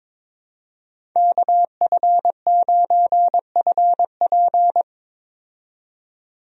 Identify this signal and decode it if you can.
Morse code, heard as a keyed tone, about 22 wpm, 700 Hz. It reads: KF9FP
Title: USA Call Signs - 5 Characters - 22wpm